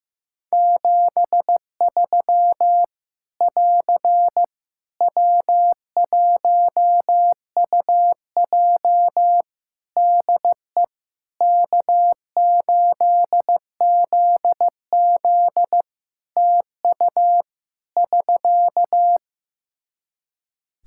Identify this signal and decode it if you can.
Morse code, heard as a keyed tone, about 15 wpm, 700 Hz. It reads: 73 <AR> W1UJ DE K8ZZ TU <SK>
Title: Practice QSO - Exchange 4 - Ending - 15wpm